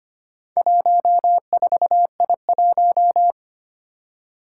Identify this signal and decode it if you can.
Morse code, heard as a keyed tone, about 25 wpm, 700 Hz. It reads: 14I1
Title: Four Challenging Letter-Number Combinations 25wpm